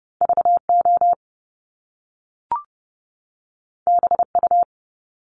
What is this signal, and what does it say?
Morse code, heard as a keyed tone, about 30 wpm, 700 Hz. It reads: VO BU
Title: Top Two Letter Combinations in Words 30wpm